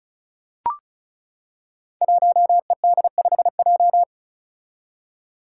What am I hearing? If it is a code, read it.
Morse code, heard as a keyed tone, about 34 wpm, 700 Hz. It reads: 1ED5J
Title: Five Challenging Letter-Number Combinations 35wpm